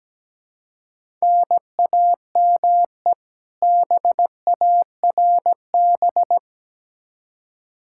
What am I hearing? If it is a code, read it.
Morse code, heard as a keyed tone, about 17 wpm, 700 Hz. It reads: NAME BARB
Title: K1USN SST QSO Practice 17wpm